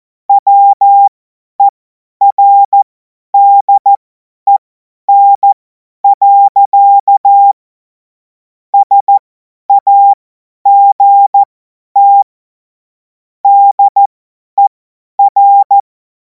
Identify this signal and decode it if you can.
Morse code, heard as a keyed tone, about 14 wpm, spaced as for 10 wpm, 800 Hz. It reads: WERDEN. SAGT DER
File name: Morsekurs / Witze / Witz 04. August 2026 50 F70